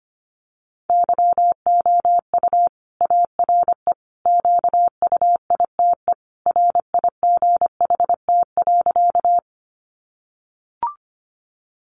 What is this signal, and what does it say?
Morse code, heard as a keyed tone, about 25 wpm, 700 Hz. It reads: YOU ARE QUITE RIGHT.
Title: Sentences from Top 700 Words 25wpm